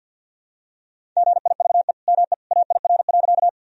Morse code, heard as a keyed tone, about 52 wpm, 700 Hz. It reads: GIVE ME AIR.
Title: Sentences from Top 1000 Words - Review Part 1 - 50wpm